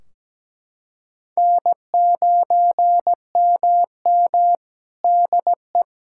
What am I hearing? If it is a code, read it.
Morse code, heard as a keyed tone, about 17 wpm, 700 Hz. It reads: N9MM DE
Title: K1USN SST QSO Practice 17wpm